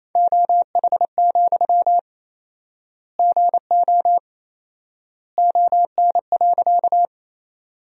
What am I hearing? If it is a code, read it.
Morse code, heard as a keyed tone, about 28 wpm, 700 Hz. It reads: OH, GO ON.
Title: 4x Word-Spacing - Sentences from Top 500 Words - 28wpm